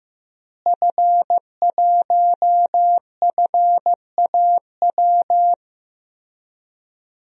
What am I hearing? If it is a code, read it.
Morse code, heard as a keyed tone, about 16 wpm, 700 Hz. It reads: F1FAW
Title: DXCC-100 Call Signs - 5 Characters - Repeated 3x - 15wpm